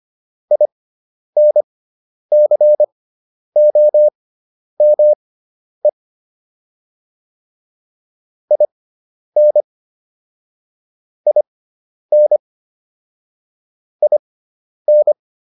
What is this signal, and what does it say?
Morse code, heard as a keyed tone, about 25 wpm, spaced as for 10 wpm, 600 Hz. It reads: INCOME IN IN IN